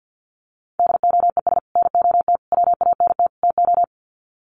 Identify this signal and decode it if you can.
Morse code, heard as a keyed tone, about 50 wpm, 700 Hz. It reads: DOES NOT WANT TO
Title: Sets of 4 Words 50wpm